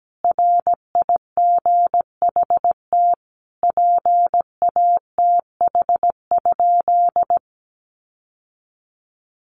Morse code, heard as a keyed tone, about 17 wpm, 700 Hz. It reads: RIGHT PATH?
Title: Sentences from Top 1400 Words 17wpm